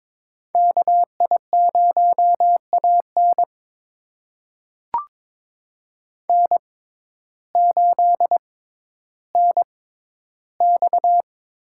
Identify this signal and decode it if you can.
Morse code, heard as a keyed tone, about 22 wpm, spaced as for 12 wpm, 700 Hz. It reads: KI0AN N8NX
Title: USA Call Signs - 5 Characters - 6x Character Spacing - 22wpm